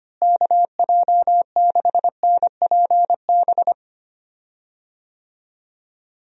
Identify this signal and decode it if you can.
Morse code, heard as a keyed tone, about 25 wpm, 700 Hz. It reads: KJ6NPB
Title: USA Call Signs - 6 Characters - Repeated 3x - 25wpm